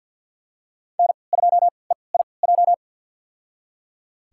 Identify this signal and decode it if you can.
Morse code, heard as a keyed tone, about 51 wpm, spaced as for 28 wpm, 700 Hz. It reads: N2EIJ
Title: USA Call Signs - 5 Characters - 3x Character Spacing - 50wpm